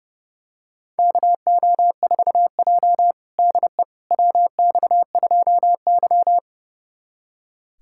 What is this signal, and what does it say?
Morse code, heard as a keyed tone, about 32 wpm, 700 Hz. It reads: KO4J DE WX2Y
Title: USA Callsign DE Callsign - 4 Characters - 30wpm